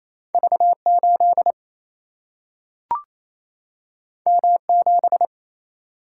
Morse code, heard as a keyed tone, about 28 wpm, 700 Hz. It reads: V8 M7
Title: Single-Letter Single-Number 28wpm